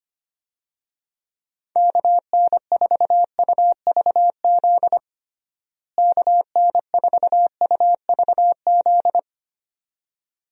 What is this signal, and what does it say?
Morse code, heard as a keyed tone, about 25 wpm, 700 Hz. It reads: KN4UVZ KN4UVZ
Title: USA Call Signs - 6 Characters - Repeated 2x - 25wpm